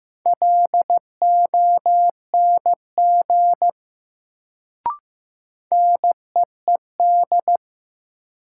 Morse code, heard as a keyed tone, about 15 wpm, 700 Hz. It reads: LONG NEED
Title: Sets of 2 Words - New Word Drill - 15wpm